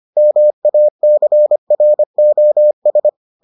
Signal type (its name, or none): Morse code